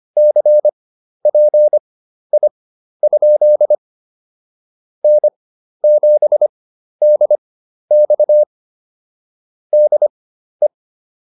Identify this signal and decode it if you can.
Morse code, heard as a keyed tone, about 25 wpm, spaced as for 12 wpm, 600 Hz. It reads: CPI? N7DX DE